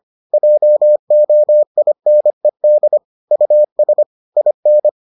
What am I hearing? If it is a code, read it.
Morse code, heard as a keyed tone, about 25 wpm, 600 Hz. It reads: JOINED US IN